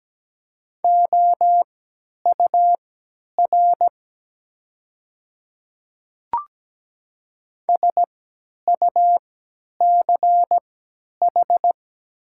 Morse code, heard as a keyed tone, about 17 wpm, spaced as for 10 wpm, 700 Hz. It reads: OUR SUCH
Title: 3x Character Spacing - Top 200 Words - 17wpm